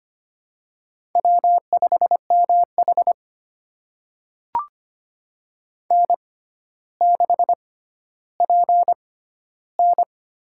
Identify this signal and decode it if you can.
Morse code, heard as a keyed tone, about 25 wpm, spaced as for 13 wpm, 700 Hz. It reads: W5MH N6PN
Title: USA Call Signs - 4 Characters - 6x Character Spacing - 25wpm